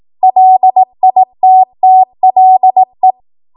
Morse code, heard as a keyed tone, about 18 wpm, 750 Hz. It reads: LITTLE
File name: Morse / 18WPM